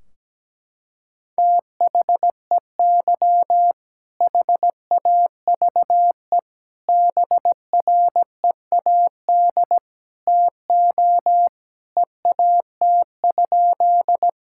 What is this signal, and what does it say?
Morse code, heard as a keyed tone, about 17 wpm, 700 Hz. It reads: THEY HAVE BREAD TO EAT?
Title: Sentences from Top 1600 Words 17wpm